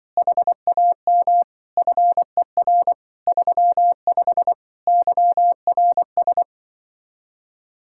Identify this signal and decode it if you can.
Morse code, heard as a keyed tone, about 24 wpm, 700 Hz. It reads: HAM FER 35 YRS
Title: Speed-Racing Common QSO Phrases - 20wpm